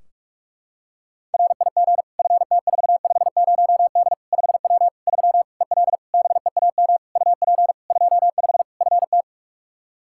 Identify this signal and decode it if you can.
Morse code, heard as a keyed tone, about 47 wpm, 700 Hz. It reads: RIG FT450D 5W 3 EL BEAM UP 25 FT
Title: Practice QSO - Exchange 2 - Info - 45wpm